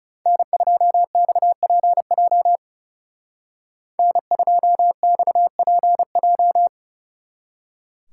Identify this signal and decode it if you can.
Morse code, heard as a keyed tone, about 36 wpm, 700 Hz. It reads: N2XPJ N2XPJ
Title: Speed Racing - USA 1x3 Call Signs 30wpm